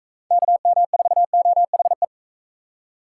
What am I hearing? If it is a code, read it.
Morse code, heard as a keyed tone, about 44 wpm, 700 Hz. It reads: KM4OHE